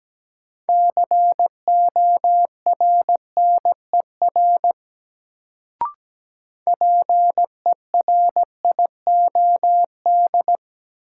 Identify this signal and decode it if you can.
Morse code, heard as a keyed tone, about 17 wpm, 700 Hz. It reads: CORNER PERIOD
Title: Sets of 5 Words - New Word Drill - 17wpm